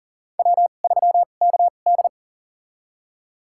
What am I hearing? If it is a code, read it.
Morse code, heard as a keyed tone, about 37 wpm, spaced as for 28 wpm, 690 Hz. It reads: W3KD
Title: USA Call Signs - 4 Characters - 2x Character Spacing - 40wpm